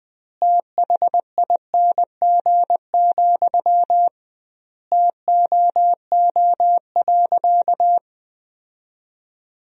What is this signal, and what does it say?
Morse code, heard as a keyed tone, about 20 wpm, 700 Hz. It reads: THING, TOO.
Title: 2x Word-Spacing - Sentences from Top 200 Words - 20wpm